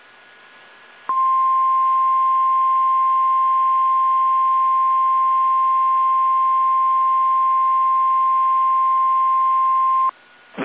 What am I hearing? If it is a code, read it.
Emergency Alert System (SAME) tones (the 1050 Hz attention tone). It is heard from 1.1 to 10.1 s.